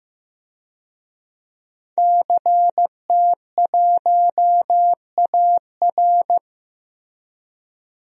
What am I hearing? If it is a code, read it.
Morse code, heard as a keyed tone, about 15 wpm, 700 Hz. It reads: CT1AR
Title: DXCC-100 Call Signs - 5 Characters - Repeated 3x - 15wpm